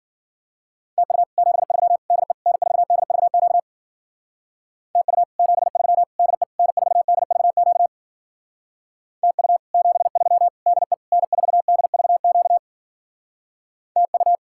Morse code, heard as a keyed tone, about 48 wpm, 700 Hz. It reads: TU 73 DE N4DVQ TU 73 DE N4DVQ TU 73 DE N4DVQ TU
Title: Speed-Racing Common QSO Phrases - 40wpm